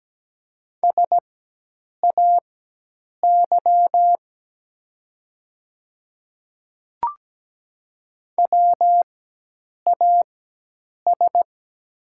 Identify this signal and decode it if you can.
Morse code, heard as a keyed tone, about 17 wpm, spaced as for 8 wpm, 700 Hz. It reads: SAY WAS
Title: 4x Character Spacing - Top 200 Words - 17wpm